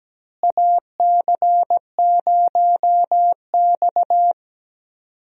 Morse code, heard as a keyed tone, about 17 wpm, 700 Hz. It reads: AC0X
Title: USA Call Signs - 4 Characters - Repeated 3x - 17wpm